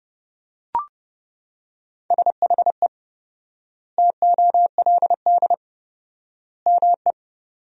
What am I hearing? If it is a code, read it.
Morse code, heard as a keyed tone, about 30 wpm, 700 Hz. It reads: SHE TOLD ME
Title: 4x Word-Spacing - Sentences from Top 400 Words - 30wpm